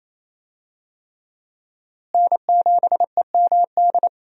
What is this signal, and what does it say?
Morse code, heard as a keyed tone, about 28 wpm, 700 Hz. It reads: N7EMD
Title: USA Call Signs - 5 Characters - Repeated 3x - 28wpm